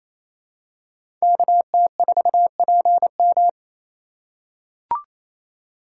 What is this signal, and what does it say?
Morse code, heard as a keyed tone, about 28 wpm, 700 Hz. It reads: KT4PM
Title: USA Call Signs - 5 Characters - 4x Character Spacing - 28wpm